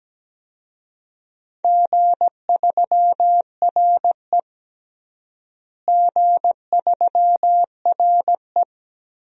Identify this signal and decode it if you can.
Morse code, heard as a keyed tone, about 17 wpm, 700 Hz. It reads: G3RE G3RE
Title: DXCC-100 Call Signs - 4 Characters - Repeated 3x - 17wpm